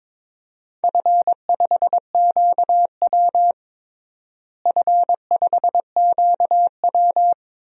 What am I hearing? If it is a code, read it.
Morse code, heard as a keyed tone, about 22 wpm, 700 Hz. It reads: F5QW F5QW
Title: DXCC-100 Call Signs - 4 Characters - Repeated 3x - 22wpm